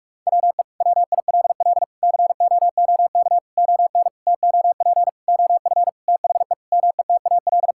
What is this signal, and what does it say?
Morse code, heard as a keyed tone, about 46 wpm, 700 Hz. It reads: WE WILL COOK ON TOP OF THE METAL
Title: Sentences from Top 1000 Words - Review Part 6 - 45wpm